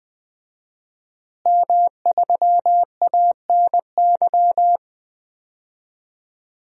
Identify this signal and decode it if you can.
Morse code, heard as a keyed tone, about 21 wpm, 700 Hz. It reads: M3ANY
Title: DXCC-100 Call Signs - 5 Characters - Repeated 3x - 20wpm